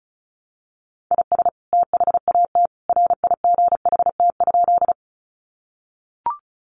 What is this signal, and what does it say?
Morse code, heard as a keyed tone, about 34 wpm, 700 Hz. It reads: IS THAT RIGHT?
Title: Sentences from Top 200 Words 35wpm